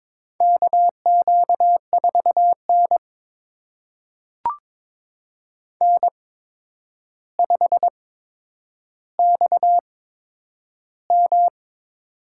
Morse code, heard as a keyed tone, about 22 wpm, spaced as for 9 wpm, 700 Hz. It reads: KQ4N N5XM
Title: USA Call Signs - 4 Characters - 8x Character Spacing - 22wpm